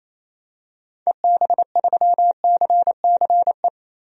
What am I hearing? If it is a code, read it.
Morse code, heard as a keyed tone, about 28 wpm, 700 Hz. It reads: EB3CCE
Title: DXCC-100 Call Signs - 6 Characters - Repeated 3x - 28wpm